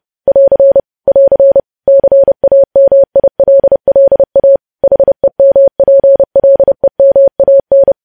Morse code, heard as a keyed tone, about 30 wpm, 550 Hz. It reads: <AR> <AR> CAMILLA HEMPLEMAN
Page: news headlines in morse code at 30 wpm